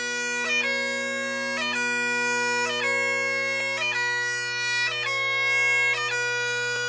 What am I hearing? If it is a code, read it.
Emergency Alert System (SAME) tones (header bursts), heard as harsh buzzing bursts. It is heard from 0.6 to 6.0 s.
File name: wrct-log hi- 15